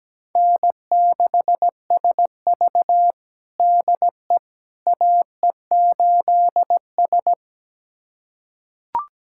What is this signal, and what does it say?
Morse code, heard as a keyed tone, about 17 wpm, 700 Hz. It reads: N6SV DE AE8S
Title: USA Callsign DE Callsign - 4 Characters - 17wpm